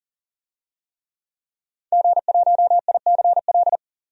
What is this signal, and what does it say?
Morse code, heard as a keyed tone, about 40 wpm, 700 Hz. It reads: G1ICL
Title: DXCC-100 Call Signs - 5 Characters - 40wpm